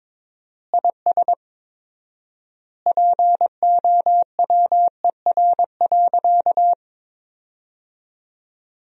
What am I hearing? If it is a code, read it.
Morse code, heard as a keyed tone, about 22 wpm, 700 Hz. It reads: IS POWER.
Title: 4x Word-Spacing - Sentences from Top 500 Words - 22wpm